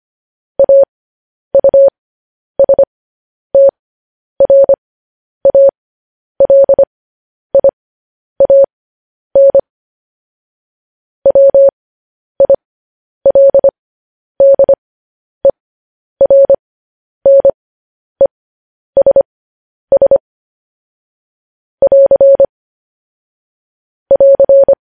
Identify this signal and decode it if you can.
Morse code, heard as a keyed tone, about 25 wpm, spaced as for 10 wpm, 550 Hz. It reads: AUSTRALIAN WILDERNESS <AR> <AR>
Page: news headlines in morse code at 10 wpm